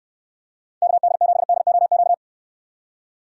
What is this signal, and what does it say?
Morse code, heard as a keyed tone, about 68 wpm, 700 Hz. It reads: KD7DOQ